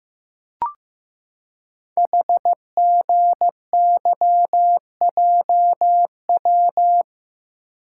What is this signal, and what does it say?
Morse code, heard as a keyed tone, about 15 wpm, 700 Hz. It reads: HGYJW
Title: Code Groups - 5 Letters - 15wpm